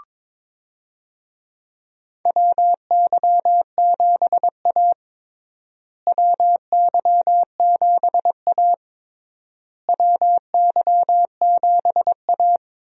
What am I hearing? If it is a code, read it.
Morse code, heard as a keyed tone, about 22 wpm, 700 Hz. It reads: WY7A WY7A WY7A
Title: USA Call Signs - 4 Characters - Repeated 3x - 22wpm